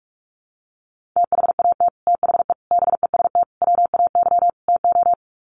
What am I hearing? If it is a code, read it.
Morse code, heard as a keyed tone, about 44 wpm, 700 Hz. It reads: THAT THE BEST WAY TO
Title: Sets of 5 Words Encore2 45wpm